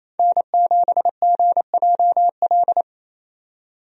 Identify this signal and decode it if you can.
Morse code, heard as a keyed tone, about 28 wpm, 700 Hz. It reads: N7GJL
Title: USA Call Signs - 5 Characters - 28wpm